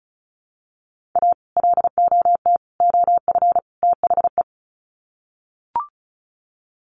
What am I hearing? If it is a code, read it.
Morse code, heard as a keyed tone, about 34 wpm, 700 Hz. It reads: A LOT OF THE
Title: Sets of 4 Words 35wpm